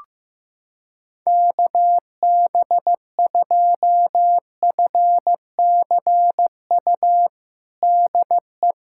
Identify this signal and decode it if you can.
Morse code, heard as a keyed tone, about 15 wpm, 700 Hz. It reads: KB2FCU DE
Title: USA Callsign DE Callsign 15wpm